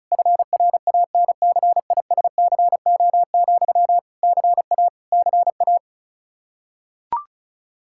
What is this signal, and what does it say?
Morse code, heard as a keyed tone, about 34 wpm, 700 Hz. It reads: FRANCISCO, CA CA
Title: Practice QSO - Exchange 1 - Beginning - 35wpm